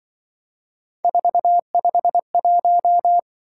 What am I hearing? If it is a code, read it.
Morse code, heard as a keyed tone, about 25 wpm, 700 Hz. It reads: 451